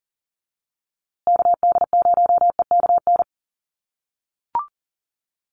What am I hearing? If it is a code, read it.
Morse code, heard as a keyed tone, about 40 wpm, 700 Hz. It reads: KD0EKN
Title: USA Call Signs Repeated 3x 40wpm